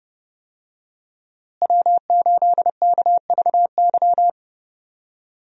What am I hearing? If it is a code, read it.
Morse code, heard as a keyed tone, about 31 wpm, 700 Hz. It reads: W8KVY
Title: USA Call Signs - 5 Characters - Repeated 2x - 30wpm